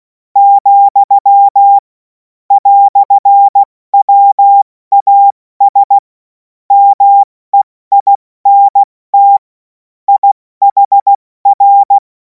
Morse code, heard as a keyed tone, about 16 wpm, 800 Hz. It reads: , "WAS MEINT IHR
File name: Morsekurs / Witze / Witz 24. Februar 2026 70 F80